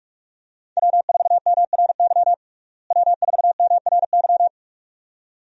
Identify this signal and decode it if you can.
Morse code, heard as a keyed tone, about 42 wpm, 700 Hz. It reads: W4MRY W4MRY
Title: USA Call Signs - 5 Characters - Repeated 2x - 45wpm